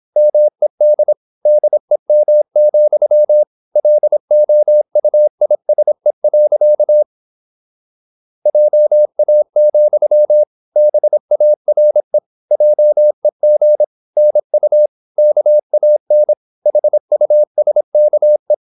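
Morse code, heard as a keyed tone, about 26 wpm, 600 Hz. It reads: MED DEM, LOUISE. JA, BARE JEG NU KAN HUSKE
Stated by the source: High-speed (QRQ) Morse at 26 wpm.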